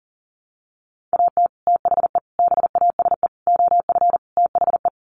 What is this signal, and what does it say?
Morse code, heard as a keyed tone, about 40 wpm, 700 Hz. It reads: AT THE BASE OF THE
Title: Sets of 5 Words 40wpm